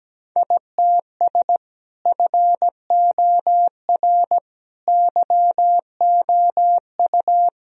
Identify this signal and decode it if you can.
Morse code, heard as a keyed tone, about 17 wpm, 700 Hz. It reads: ITS FOR YOU
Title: Sentences from Top 1000 Words - Review Part 3 - 17wpm